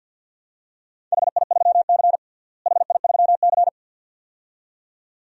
Morse code, heard as a keyed tone, about 52 wpm, 700 Hz. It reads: HI3C HI3C
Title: DXCC-100 Call Signs - 4 Characters - Repeated 2x - 50wpm